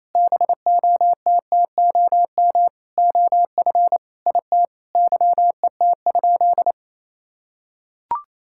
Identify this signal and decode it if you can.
Morse code, heard as a keyed tone, about 28 wpm, 700 Hz. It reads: BOTTOM OF IT YET?
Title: Sentences from Top 1000 Words - Review Part 6 - 28wpm